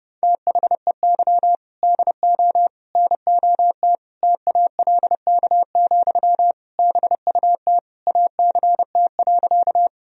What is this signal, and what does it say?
Morse code, heard as a keyed tone, about 30 wpm, 700 Hz. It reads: THEY DO NOT TALK, BUT ACT.
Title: Sentences from Top 1000 Words - Review Part 1 - 30wpm